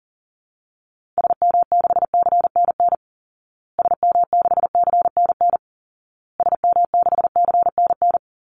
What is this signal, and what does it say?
Morse code, heard as a keyed tone, about 40 wpm, 700 Hz. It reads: SM6CNN SM6CNN SM6CNN
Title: Call Signs Repeated 2x - 40wpm